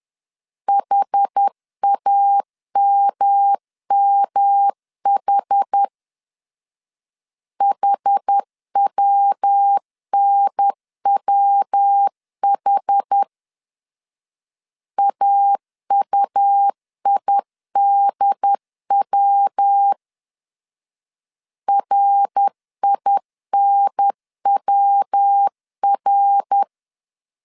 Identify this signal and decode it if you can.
Morse code, heard as a keyed tone, about 10 wpm, 800 Hz. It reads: HAMMH HWNWH AUIDW RINWR
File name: CW / DET / 1 3 1500C S B 10wpm 3335